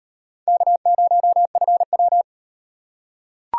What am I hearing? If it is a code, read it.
Morse code, heard as a keyed tone, about 34 wpm, 700 Hz. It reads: K0FW